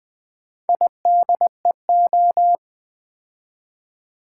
Morse code, heard as a keyed tone, about 20 wpm, 700 Hz. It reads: IDEO
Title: Build A Word - 2 Letters At A Time - 8-Letter Words - 20wpm